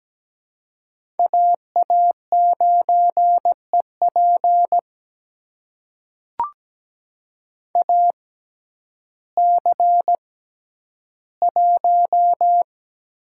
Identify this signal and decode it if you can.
Morse code, heard as a keyed tone, about 17 wpm, spaced as for 10 wpm, 700 Hz. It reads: AA9EP AC1
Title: USA Call Signs - 6x Character Spacing - 17wpm